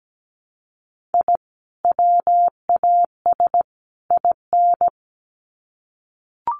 Morse code, heard as a keyed tone, about 17 wpm, 700 Hz. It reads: I WAS IN
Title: Sets of 3 Words 17wpm